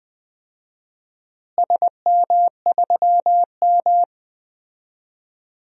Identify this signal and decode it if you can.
Morse code, heard as a keyed tone, about 20 wpm, 700 Hz. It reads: SM3M
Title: DXCC-100 Call Signs - 4 Characters - Repeated 3x - 20wpm